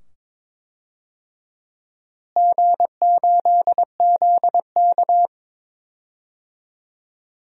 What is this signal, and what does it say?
Morse code, heard as a keyed tone, about 22 wpm, 700 Hz. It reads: G8ZK
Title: DXCC-100 Call Signs - 4 Characters - Repeated 2x - 22wpm